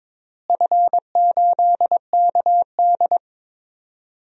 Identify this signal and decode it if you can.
Morse code, heard as a keyed tone, about 22 wpm, 700 Hz. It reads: F8KD
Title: DXCC-100 Call Signs - 4 Characters - Repeated 2x - 22wpm